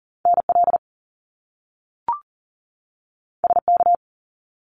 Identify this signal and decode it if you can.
Morse code, heard as a keyed tone, about 40 wpm, 700 Hz. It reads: NL SK
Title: Canadian Provinces and Territories 40wpm